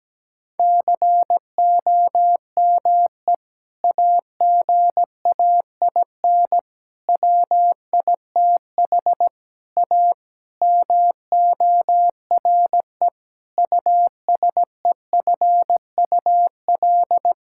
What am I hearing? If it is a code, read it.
Morse code, heard as a keyed tone, about 17 wpm, 700 Hz. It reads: COME AGAIN WITH A MORE USEFUL
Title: Sentences from Top 1000 Words - Review Part 6 - 17wpm